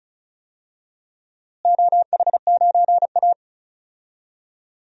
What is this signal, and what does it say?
Morse code, heard as a keyed tone, about 36 wpm, 700 Hz. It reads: OH9A
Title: DXCC-100 Call Signs - 4 Characters - Repeated 2x - 35wpm